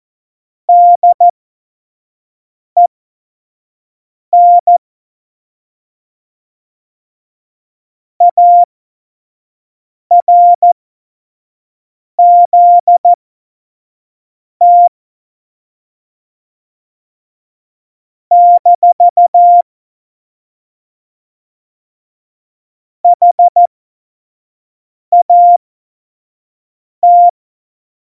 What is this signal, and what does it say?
Morse code, heard as a keyed tone, about 14 wpm, spaced as for 5 wpm, 700 Hz. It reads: DEN ARZT - HAT
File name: Morsekurs / Witze / Witz 20 March 2018, 25 F70